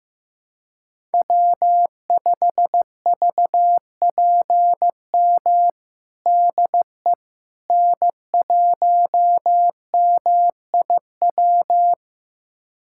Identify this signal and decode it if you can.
Morse code, heard as a keyed tone, about 16 wpm, 700 Hz. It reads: W5VPM DE N1MIW
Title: USA Callsign DE Callsign - 5 Characters - 15wpm